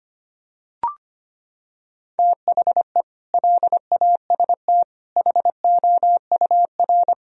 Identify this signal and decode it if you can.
Morse code, heard as a keyed tone, about 25 wpm, 700 Hz. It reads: THE LAST HOUR
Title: Sentences from Top 1000 Words - Review Part 4 - 25wpm